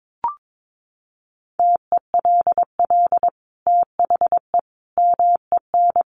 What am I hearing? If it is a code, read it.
Morse code, heard as a keyed tone, about 22 wpm, 700 Hz. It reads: TELL THE MEN
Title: Sentences from Top 1400 Words 22wpm